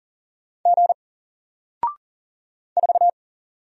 Morse code, heard as a keyed tone, about 40 wpm, 700 Hz. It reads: G 4
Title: Single Letter-Number Alt 40wpm